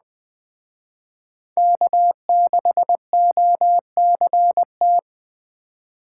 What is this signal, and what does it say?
Morse code, heard as a keyed tone, about 22 wpm, 700 Hz. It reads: K6OCT